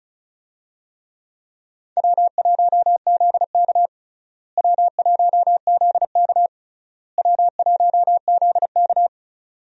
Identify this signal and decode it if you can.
Morse code, heard as a keyed tone, about 34 wpm, 700 Hz. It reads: W1ZK W1ZK W1ZK
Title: USA Call Signs - 4 Characters - Repeated 3x - 35wpm